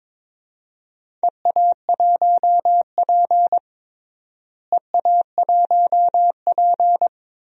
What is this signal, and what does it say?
Morse code, heard as a keyed tone, about 22 wpm, 700 Hz. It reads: EA1P EA1P
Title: DXCC-100 Call Signs - 4 Characters - Repeated 3x - 22wpm